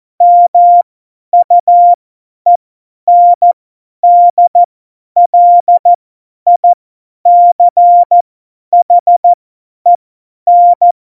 Morse code, heard as a keyed tone, about 14 wpm, spaced as for 10 wpm, 700 Hz. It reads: MUENDLICHEN